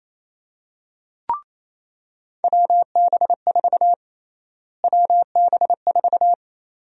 Morse code, heard as a keyed tone, about 28 wpm, 700 Hz. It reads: WB4 WB4
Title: USA Call Sign Prefixes - Repeated 2x - 28wpm